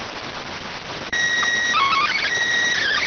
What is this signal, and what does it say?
SSTV (slow-scan television), PD 290.